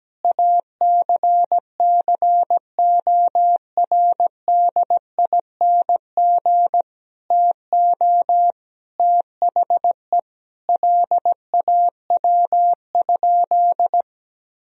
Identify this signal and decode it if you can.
Morse code, heard as a keyed tone, about 17 wpm, 700 Hz. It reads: ACCORDING TO THE LAW?
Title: Sentences from Top 1000 Words - Review Part 4 - 17wpm